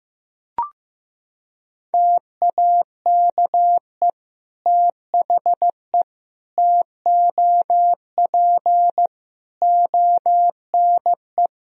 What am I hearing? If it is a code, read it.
Morse code, heard as a keyed tone, about 15 wpm, 700 Hz. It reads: TAKE THE TOP ONE
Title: Sentences from Top 1000 Words - Review Part 2 - 15wpm